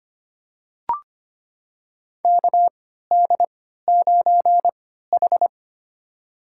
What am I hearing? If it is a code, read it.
Morse code, heard as a keyed tone, about 25 wpm, spaced as for 14 wpm, 700 Hz. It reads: KD9H
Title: USA Call Signs - 4 Characters - 3x Character Spacing - 25wpm